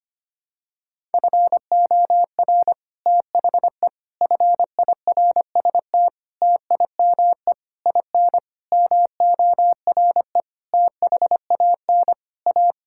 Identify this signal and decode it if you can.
Morse code, heard as a keyed tone, about 25 wpm, 700 Hz. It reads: FOR THE FIRST TIME IN MORE THAN A
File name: Sets of 9 Words - 25wpm